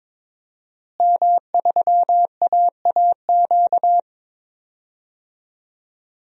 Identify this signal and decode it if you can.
Morse code, heard as a keyed tone, about 22 wpm, 700 Hz. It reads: M3AAQ
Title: DXCC-100 Call Signs - 5 Characters - Repeated 3x - 22wpm